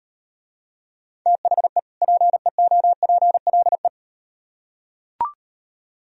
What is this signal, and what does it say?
Morse code, heard as a keyed tone, about 38 wpm, 700 Hz. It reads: THE PEOPLE